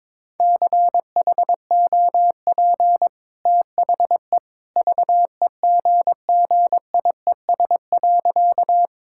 Morse code, heard as a keyed tone, about 22 wpm, 690 Hz. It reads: CHOP THE VEGGIES.
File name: practice / 22wpm / Easy Sentences with Hints - 22wpm